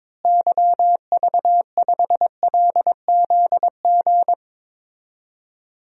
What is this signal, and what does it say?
Morse code, heard as a keyed tone, about 22 wpm, 700 Hz. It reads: YV5LZG
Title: DXCC-100 Call Signs - 6 Characters - Repeated 3x - 22wpm